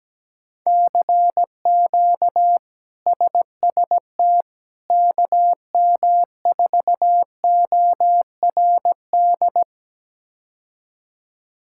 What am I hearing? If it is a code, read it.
Morse code, heard as a keyed tone, about 17 wpm, 700 Hz. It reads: CQ SST KM4ORD
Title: K1USN SST QSO Practice - 17wpm